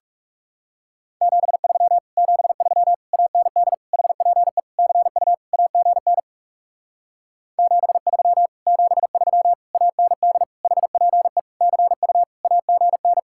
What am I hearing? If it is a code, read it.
Morse code, heard as a keyed tone, about 37 wpm, 700 Hz. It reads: 73 73 AND HPE CU AGN 73 73 AND HPE CU AGN
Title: Speed-Racing Common QSO Phrases - 30wpm